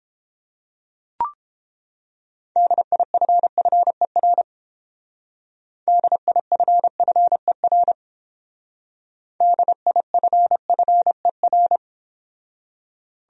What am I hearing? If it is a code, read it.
Morse code, heard as a keyed tone, about 30 wpm, 700 Hz. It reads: DIFFER DIFFER DIFFER